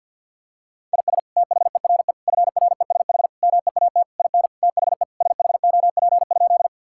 Morse code, heard as a keyed tone, about 50 wpm, 700 Hz. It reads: IS THERE FRESH MEAT IN THE SHOP?
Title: Sentences from Top 1600 Words 50wpm